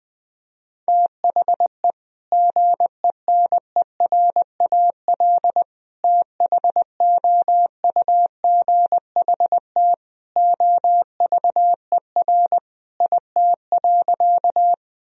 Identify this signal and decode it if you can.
Morse code, heard as a keyed tone, about 20 wpm, 700 Hz. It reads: THE GENERAL THOUGHT OVER IT.
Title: Sentences from Top 1000 Words - Review Part 3 - 20wpm